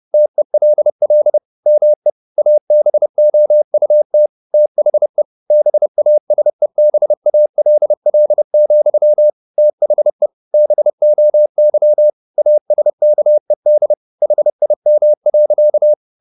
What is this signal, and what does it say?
Morse code, heard as a keyed tone, about 30 wpm, 600 Hz. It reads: TELL ME ABOUT THE BASEBALL, THE BOY ASKED HIM.
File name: the old man and the sea / the old man and the sea 1 30wpm